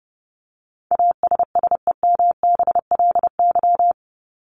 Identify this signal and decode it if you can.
Morse code, heard as a keyed tone, about 30 wpm, 700 Hz. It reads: ASSEMBLY
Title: LY Suffix - Top 100 Words with Base Words 30wpm